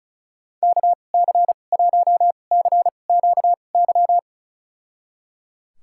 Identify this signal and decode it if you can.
Morse code, heard as a keyed tone, about 34 wpm, spaced as for 25 wpm, 700 Hz. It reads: KC1CQY
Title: USA Call Signs - 6 Characters - 2x Character Spacing - 35wpm